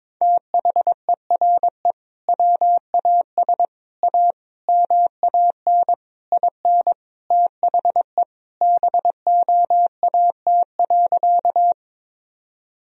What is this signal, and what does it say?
Morse code, heard as a keyed tone, about 21 wpm, 700 Hz. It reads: THERE WAS A MAN IN THE BOAT.
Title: Sentences from Top 1000 Words - Review Part 3 - 22wpm